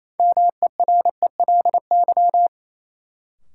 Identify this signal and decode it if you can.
Morse code, heard as a keyed tone, about 28 wpm, 700 Hz. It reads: MERELY
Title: Sentences from Top 1400 Words - New Word Drill Alt - 28wpm